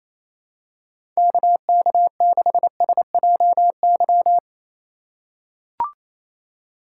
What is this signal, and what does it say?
Morse code, heard as a keyed tone, about 27 wpm, 700 Hz. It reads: KK6SJY
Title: USA Call Signs - 6 Characters - 2x Character Spacing - 28wpm